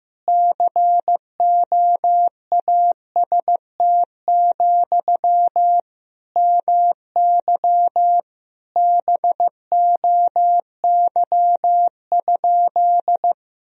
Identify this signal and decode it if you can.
Morse code, heard as a keyed tone, about 16 wpm, 700 Hz. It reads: COAST, MY BOY?
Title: Sentences from Top 1000 Words - Review Part 4 - 15wpm